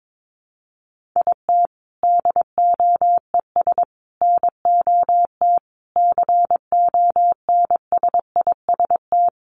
Morse code, heard as a keyed tone, about 22 wpm, 700 Hz. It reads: IT DOES NOT CONSIST
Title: Sentences from Top 1400 Words 22wpm